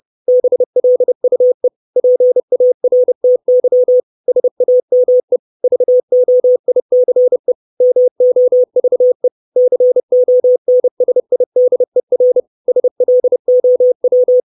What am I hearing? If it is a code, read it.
Morse code, heard as a keyed tone, about 30 wpm, 500 Hz. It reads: BLUE PARTY SAME VOICE MOVE CONSIDER SLOW
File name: 632 Words 30WPM